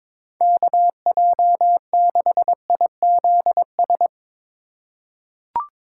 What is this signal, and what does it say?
Morse code, heard as a keyed tone, about 22 wpm, 700 Hz. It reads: KJ6IZS
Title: USA Call Signs - 6 Characters - 3x Character Spacing - 22wpm